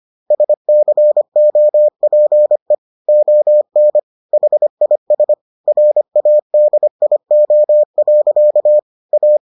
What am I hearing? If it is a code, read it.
Morse code, heard as a keyed tone, about 25 wpm, 600 Hz. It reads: SCOPE ON HIS RADIO. A